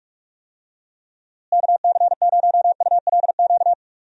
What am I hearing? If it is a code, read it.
Morse code, heard as a keyed tone, about 45 wpm, 700 Hz. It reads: KC0ULQ
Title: USA Call Signs - 6 Characters - Repeated 3x - 45wpm